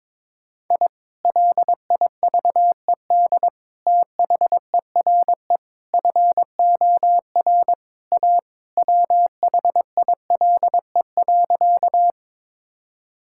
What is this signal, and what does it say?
Morse code, heard as a keyed tone, about 22 wpm, 700 Hz. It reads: I LIVED THERE FOR A WHILE.
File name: practice / 22wpm / Sentences with 3-Word Phrases - 22wpm